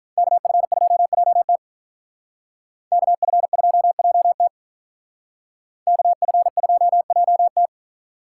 Morse code, heard as a keyed tone, about 42 wpm, 700 Hz. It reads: KF2JT KF2JT KF2JT
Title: Speed Racing - USA 2x2 Call Signs 35wpm